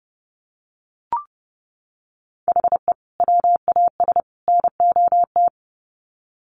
Morse code, heard as a keyed tone, about 30 wpm, 700 Hz. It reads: HE WAS NOT
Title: Sets of 3 Words 30wpm